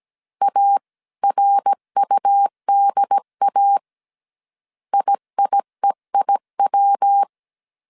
Morse code, heard as a keyed tone, about 17 wpm, 800 Hz. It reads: ARUDA IIEIW